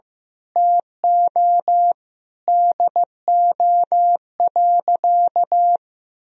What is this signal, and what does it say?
Morse code, heard as a keyed tone, about 15 wpm, 700 Hz. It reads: TO DO.
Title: Sentences from Top 1000 Words - Review Part 2 - 15wpm